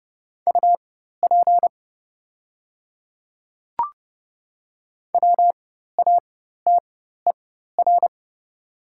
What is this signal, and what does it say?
Morse code, heard as a keyed tone, about 30 wpm, spaced as for 14 wpm, 690 Hz. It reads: UP WATER
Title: 4x Character Spacing - Top 300 Words - 30wpm